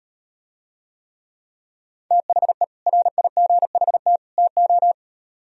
Morse code, heard as a keyed tone, about 38 wpm, 700 Hz. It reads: THE RIGHT TO